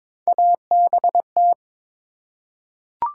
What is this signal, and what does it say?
Morse code, heard as a keyed tone, about 23 wpm, 700 Hz. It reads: ABT
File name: Top 90 QSO Elements - 4x Character Spacing - 22wpm